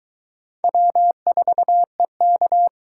Morse code, heard as a keyed tone, about 26 wpm, 700 Hz. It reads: W4EK